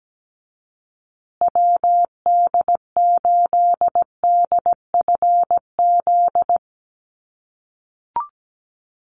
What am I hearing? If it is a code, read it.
Morse code, heard as a keyed tone, about 17 wpm, 700 Hz. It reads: WD8DFZ
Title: USA Call Signs Repeated 2x 17wpm